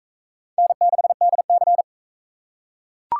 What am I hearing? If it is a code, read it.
Morse code, heard as a keyed tone, about 44 wpm, 700 Hz. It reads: N6DC